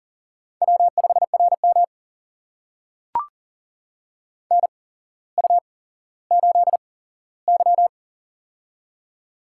Morse code, heard as a keyed tone, about 41 wpm, spaced as for 17 wpm, 700 Hz. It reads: W5RM NU8Y
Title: USA Call Signs - 4 Characters - 8x Character Spacing - 40wpm